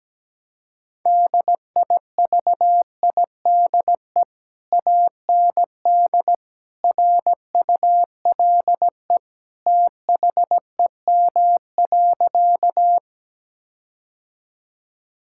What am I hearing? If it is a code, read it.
Morse code, heard as a keyed tone, about 17 wpm, 700 Hz. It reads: DIVIDE AND RULE THEM.
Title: Sentences from Top 1600 Words 17wpm